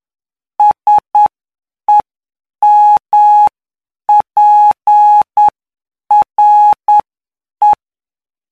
Morse code, heard as a keyed tone, about 9 wpm, spaced as for 7 wpm, 840 Hz. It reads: SEMPRE
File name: cw / py2lcd / 19-PY2LCD-Li��o 28